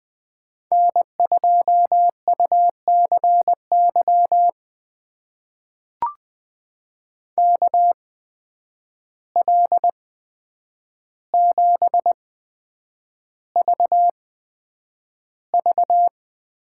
Morse code, heard as a keyed tone, about 20 wpm, spaced as for 9 wpm, 700 Hz. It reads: N2UCY KL7VV
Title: USA Call Signs - 5 Characters - 8x Character Spacing - 20wpm